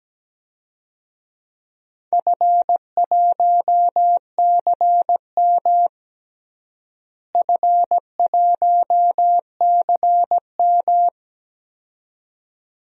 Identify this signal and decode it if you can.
Morse code, heard as a keyed tone, about 17 wpm, 700 Hz. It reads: F1CM F1CM
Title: DXCC-100 Call Signs - 4 Characters - Repeated 2x - 17wpm